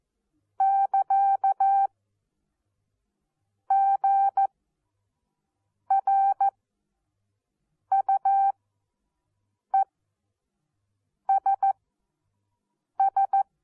Morse code, heard as a keyed tone, about 14 wpm, spaced as for 5 wpm, 780 Hz. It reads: <KA> GRUESS
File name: Onlinelehrgang / morsen / DJ4UF Morsen14